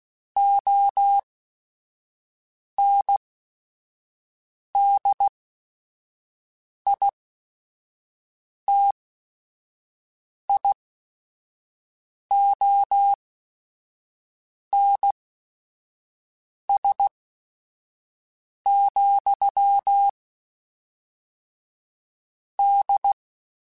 Morse code, heard as a keyed tone, about 16 wpm, spaced as for 5 wpm, 800 Hz. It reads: ONDITIONS, D
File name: Technical Subjects / Technical Subjects-mp3s / AND Gate @5